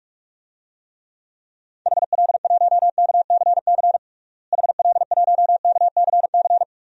Morse code, heard as a keyed tone, about 46 wpm, 700 Hz. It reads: HL1KCC HL1KCC
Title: DXCC-100 Call Signs - 6 Characters - Repeated 3x - 45wpm